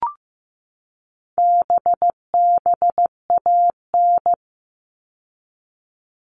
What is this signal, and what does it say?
Morse code, heard as a keyed tone, about 15 wpm, 700 Hz. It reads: BBAN
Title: All Cut Numbers - 4-Digit 15wpm